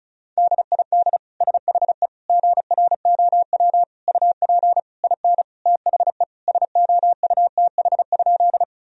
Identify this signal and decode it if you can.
Morse code, heard as a keyed tone, about 34 wpm, 700 Hz. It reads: DID SHE GROW UP IN THE SOUTH?
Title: Sentences from Top 1000 Words - Review Part 2 - 35wpm